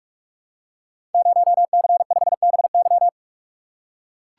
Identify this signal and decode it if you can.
Morse code, heard as a keyed tone, about 44 wpm, 700 Hz. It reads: 0C5BY
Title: Code Groups - 5 Letter:Numbers - 45wpm